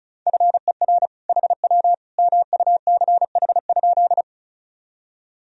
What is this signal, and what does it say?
Morse code, heard as a keyed tone, about 34 wpm, 690 Hz. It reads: FER HW MUCH?
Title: Sentences from Top 200 Words and Abbreviations 35wpm